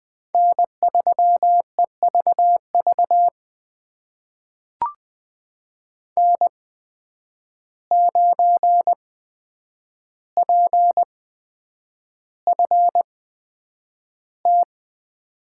Morse code, heard as a keyed tone, about 20 wpm, spaced as for 9 wpm, 700 Hz. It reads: N3EVV N9PFT
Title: USA Call Signs - 5 Characters - 8x Character Spacing - 20wpm